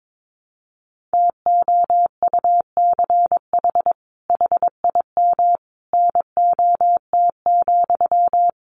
Morse code, heard as a keyed tone, about 22 wpm, 700 Hz. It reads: TOUCH HIM NOT,
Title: Sentences from Top 1200 Words 22wpm